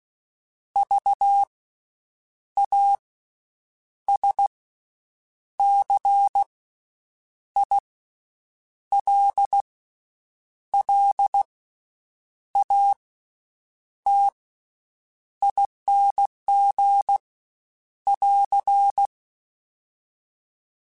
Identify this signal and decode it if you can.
Morse code, heard as a keyed tone, about 16 wpm, 800 Hz. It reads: V A S C I L L A T ING <AR>
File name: Special INGs-mp3s / Special INGs-mp3s / Slow ing @15